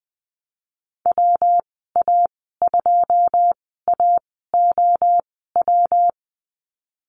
Morse code, heard as a keyed tone, about 20 wpm, spaced as for 14 wpm, 700 Hz. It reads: WA2AOW
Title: 147 Lesson 24g - Letter y - Call Signs 20wpm